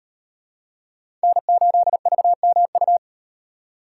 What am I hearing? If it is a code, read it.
Morse code, heard as a keyed tone, about 36 wpm, 700 Hz. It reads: N8VMU